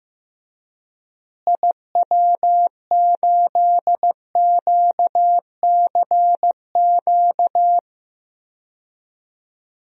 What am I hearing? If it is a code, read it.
Morse code, heard as a keyed tone, about 15 wpm, 700 Hz. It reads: IW8QCQ
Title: DXCC-100 Call Signs - 6 Characters - Repeated 3x - 15wpm